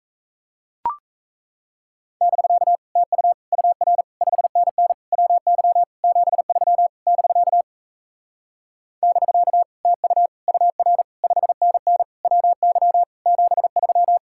Morse code, heard as a keyed tone, about 38 wpm, 700 Hz. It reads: <BK> TU UR 5NN WY 73 <BK> <BK> TU UR 5NN WY 73